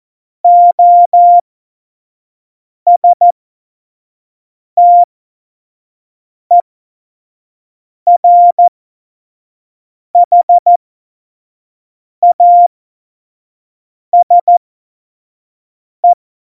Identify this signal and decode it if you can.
Morse code, heard as a keyed tone, about 14 wpm, spaced as for 5 wpm, 700 Hz. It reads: OSTERHASE
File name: Morsekurs / Witze / Witz 07 April 2020, 25 F70